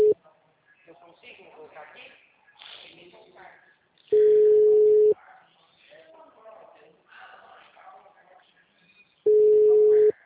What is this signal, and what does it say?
Ringback tone (Europe, 425 Hz), with speech.
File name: out- -2011- - - .50333